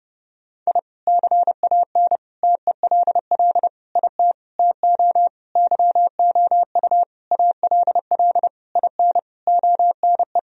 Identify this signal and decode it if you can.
Morse code, heard as a keyed tone, about 32 wpm, 700 Hz. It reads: I CAN TELL IT TO YOU ALL IN ONE
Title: Sentences from Top 1000 Words - Review Part 1 - 30wpm